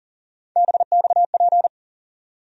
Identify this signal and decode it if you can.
Morse code, heard as a keyed tone, about 37 wpm, 700 Hz. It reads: BXP
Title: USA Call Sign Postfixes - 3 Letters 40wpm